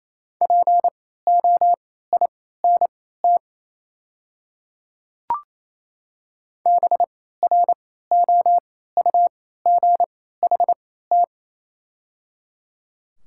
Morse code, heard as a keyed tone, about 28 wpm, spaced as for 16 wpm, 700 Hz. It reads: POINT BROUGHT
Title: 3x Character Spacing - Top 500 Words - 28wpm